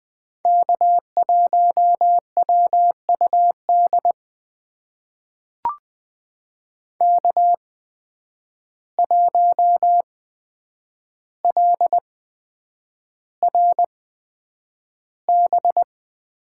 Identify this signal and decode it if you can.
Morse code, heard as a keyed tone, about 20 wpm, spaced as for 9 wpm, 700 Hz. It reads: K1WUD K1LRB
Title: USA Call Signs - 5 Characters - 8x Character Spacing - 20wpm